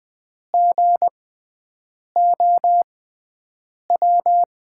Morse code, heard as a keyed tone, about 20 wpm, spaced as for 7 wpm, 700 Hz. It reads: GOW
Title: USA Call Signs - 6 Characters - 6x Character Spacing - 20wpm